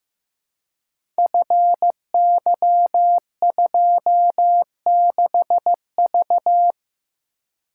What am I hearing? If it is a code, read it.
Morse code, heard as a keyed tone, about 15 wpm, 700 Hz. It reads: FY26V
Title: Code Groups - 5 Letter:Numbers - 15wpm